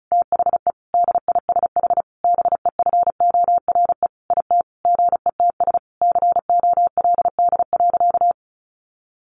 Morse code, heard as a keyed tone, about 34 wpm, 700 Hz. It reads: THE DISH BEFORE IT GETS COLD.
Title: Sentences from Top 1300 Words 35wpm